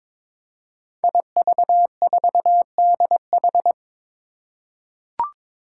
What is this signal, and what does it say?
Morse code, heard as a keyed tone, about 22 wpm, 700 Hz. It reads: IV4DH
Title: Five Challenging Letter-Number Combinations 22wpm